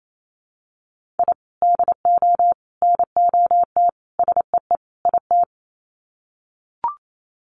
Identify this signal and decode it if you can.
Morse code, heard as a keyed tone, about 28 wpm, 700 Hz. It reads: I DO NOT SEE IT
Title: Sets of 5 Words Encore2 28wpm